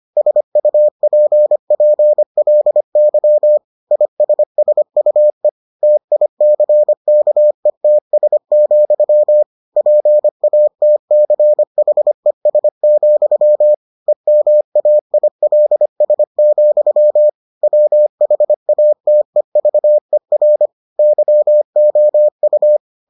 Morse code, heard as a keyed tone, about 25 wpm, 600 Hz. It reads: SUPPLY ISSUE TICKETS, PATCHES, EMAILS, WHATEVER YOU